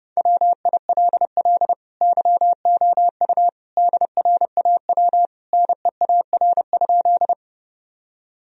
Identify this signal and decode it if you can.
Morse code, heard as a keyed tone, about 32 wpm, 700 Hz. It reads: WILL YOU DRAW NEAR?
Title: Sentences from Top 1000 Words - Review Part 1 - 30wpm